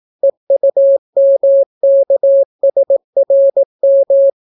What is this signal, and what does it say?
Morse code, heard as a keyed tone, about 18 wpm, 550 Hz. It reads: EUMKSRM